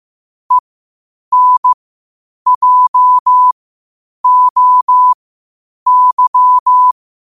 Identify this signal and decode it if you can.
Morse code, heard as a keyed tone, about 15 wpm, spaced as for 8 wpm, 1000 Hz. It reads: ENJOY
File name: qotdfiles / 8wpm